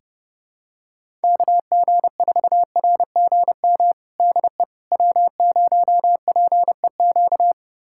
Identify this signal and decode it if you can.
Morse code, heard as a keyed tone, about 32 wpm, 700 Hz. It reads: KG4RGM DE W0PEQ
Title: USA Callsign DE Callsign 30wpm